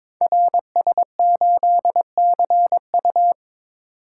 Morse code, heard as a keyed tone, about 22 wpm, 700 Hz. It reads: RS8CU
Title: Code Groups - 5 Letter:Numbers - 22wpm